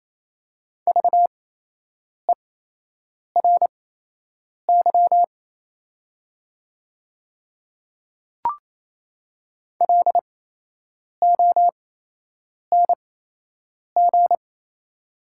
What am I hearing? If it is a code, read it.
Morse code, heard as a keyed tone, about 28 wpm, spaced as for 8 wpm, 700 Hz. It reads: VERY LONG
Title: 8x Character Spacing - Top 300 Words - 28wpm